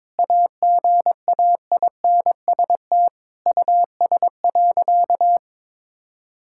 Morse code, heard as a keyed tone, about 22 wpm, 700 Hz. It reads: AGAINST US.
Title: Sentences from Top 1000 Words - Review Part 6 - 22wpm